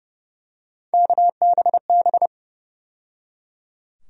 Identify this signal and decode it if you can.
Morse code, heard as a keyed tone, about 31 wpm, 700 Hz. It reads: KBB
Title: USA Call Sign Postfixes - 3 Letters 30wpm